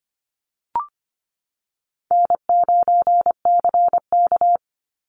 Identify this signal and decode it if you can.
Morse code, heard as a keyed tone, about 25 wpm, 700 Hz. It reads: N9CK
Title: Call Signs Repeated 3x - 25wpm